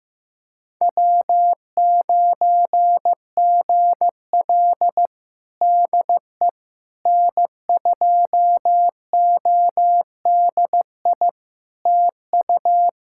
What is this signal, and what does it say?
Morse code, heard as a keyed tone, about 15 wpm, 700 Hz. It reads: W9GL DE N2ODI TU
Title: Practice QSO - Exchange 4 - Ending - 15wpm